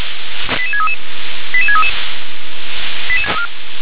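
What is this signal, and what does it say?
Five-tone selective call (ZVEI), heard as quick runs of five tones, tones of about 70 ms. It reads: 88633 88633
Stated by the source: ZVEI five-tone selective call received on an SDR.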